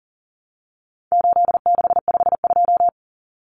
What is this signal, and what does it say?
Morse code, heard as a keyed tone, about 40 wpm, 700 Hz. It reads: 8652
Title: Common Cut Numbers - 4-Digit 40wpm